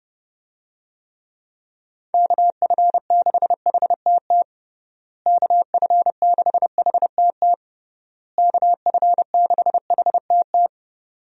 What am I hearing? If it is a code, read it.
Morse code, heard as a keyed tone, about 32 wpm, 700 Hz. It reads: KF6HTT KF6HTT KF6HTT
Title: USA Call Signs - 6 Characters - Repeated 3x - 30wpm